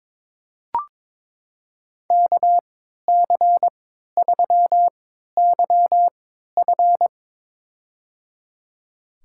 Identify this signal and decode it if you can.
Morse code, heard as a keyed tone, about 22 wpm, spaced as for 12 wpm, 700 Hz. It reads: KC3YF
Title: USA Call Signs - 3x Character Spacing - 22wpm